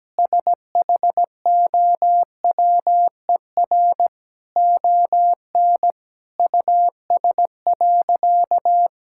Morse code, heard as a keyed tone, about 17 wpm, 700 Hz. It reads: SHOWER ON US.
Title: Sentences from Top 1600 Words 17wpm